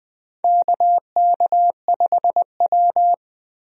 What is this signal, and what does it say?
Morse code, heard as a keyed tone, about 20 wpm, 700 Hz. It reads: KK5W
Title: USA Call Signs - 4 Characters - Repeated 2x - 20wpm